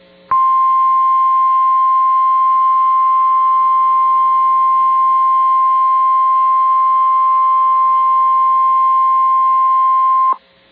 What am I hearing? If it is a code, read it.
Emergency Alert System (SAME) tones (the 1050 Hz attention tone). It is heard from 0.3 to 10.4 s.